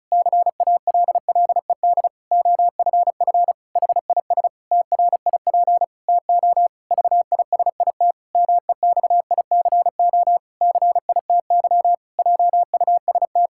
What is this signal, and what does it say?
Morse code, heard as a keyed tone, about 34 wpm, 700 Hz. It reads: CALLED OFF HIS TRIP TO VISIT MEXICO CITY JUST
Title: Pangram Challenge v2 - 35wpm